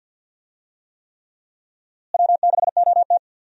Morse code, heard as a keyed tone, about 50 wpm, 700 Hz. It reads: W6OT
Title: USA Call Signs - 4 Characters - Repeated 3x - 50wpm